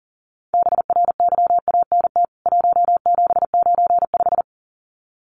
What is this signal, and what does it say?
Morse code, heard as a keyed tone, about 40 wpm, 700 Hz. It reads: BRYANT 1795
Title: CWT Exchange 40wpm